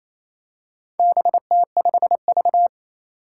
Morse code, heard as a keyed tone, about 28 wpm, 700 Hz. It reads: BT5V
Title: Four Challenging Letter-Number Combinations 28wpm